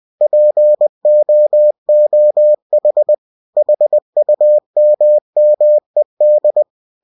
Morse code, heard as a keyed tone, about 20 wpm, 600 Hz. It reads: POOH HUMMED
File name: cw-books / winnie-the-pooh / winnie-20wpm-page-0046